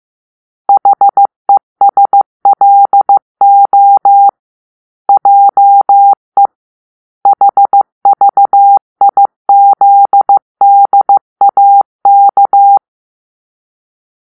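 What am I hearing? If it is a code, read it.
Morse code, heard as a keyed tone, about 15 wpm, 800 Hz. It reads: HESLO JE HVIZDAK
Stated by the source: Morse-code cipher made by a player team for the Kachna puzzle game.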